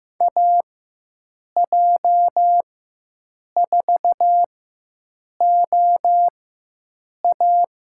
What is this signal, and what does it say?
Morse code, heard as a keyed tone, about 15 wpm, spaced as for 7 wpm, 700 Hz. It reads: AJ4OA
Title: USA Call Signs - 5 Characters - 4x Character Spacing - 15wpm